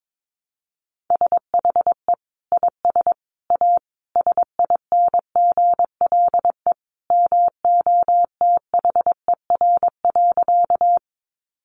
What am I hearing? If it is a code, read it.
Morse code, heard as a keyed tone, about 22 wpm, 700 Hz. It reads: SHE IS A SINGLE MOTHER.
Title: Sentences from Top 700 Words 22wpm